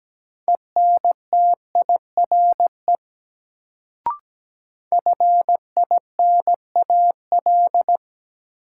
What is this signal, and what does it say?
Morse code, heard as a keyed tone, about 17 wpm, 700 Hz. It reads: ENTIRE FINAL
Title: Sentences from Top 800 Words - New Word Drill - 17wpm